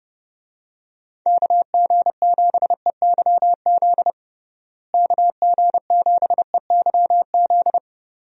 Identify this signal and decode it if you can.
Morse code, heard as a keyed tone, about 30 wpm, 700 Hz. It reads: KG7EYZ KG7EYZ
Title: USA Call Signs - 6 Characters - Repeated 3x - 30wpm